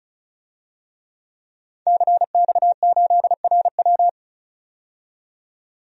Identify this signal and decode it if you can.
Morse code, heard as a keyed tone, about 36 wpm, 700 Hz. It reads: CX8RW
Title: DXCC-100 Call Signs - 5 Characters - 35wpm